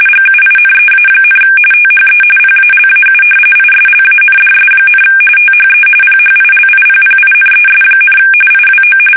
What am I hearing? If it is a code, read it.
RTTY, about 45.45 baud. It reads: GUINEA, THE PALAU ISLANDS, THE PHILIPPINES, IWO
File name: fsk850 / battleshipnj